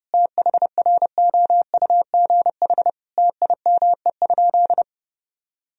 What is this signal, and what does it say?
Morse code, heard as a keyed tone, about 32 wpm, 700 Hz. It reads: THROUGH TIME?
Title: Sentences from Top 1000 Words - Review Part 2 - 30wpm